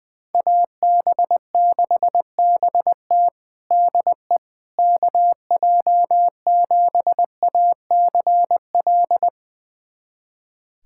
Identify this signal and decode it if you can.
Morse code, heard as a keyed tone, about 21 wpm, 700 Hz. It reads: AB6BT DE KJ7ACL
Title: USA Callsign DE Callsign 20wpm